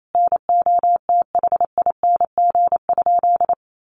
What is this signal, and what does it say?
Morse code, heard as a keyed tone, about 28 wpm, 700 Hz. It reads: NOTHING?
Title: Sentences from Top 500 Words 28wpm